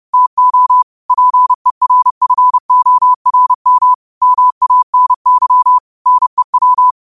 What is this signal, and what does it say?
Morse code, heard as a keyed tone, about 30 wpm, 1000 Hz. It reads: TO PERFORM MANY NEW